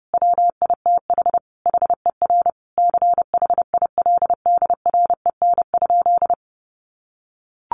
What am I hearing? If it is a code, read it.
Morse code, heard as a keyed tone, about 30 wpm, 700 Hz. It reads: WITH HER CHILDREN?
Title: Sentences from Top 1500 Words 30wpm